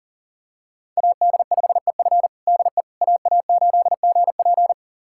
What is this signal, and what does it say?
Morse code, heard as a keyed tone, about 40 wpm, 700 Hz. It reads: AD5EF DE AA8GP
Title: USA Callsign DE Callsign - 5 Characters - 40wpm